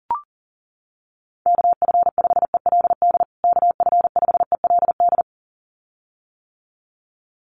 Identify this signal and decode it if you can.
Morse code, heard as a keyed tone, about 40 wpm, 700 Hz. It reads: KF5ELD KF5ELD
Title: USA Call Signs Repeated 2x 40wpm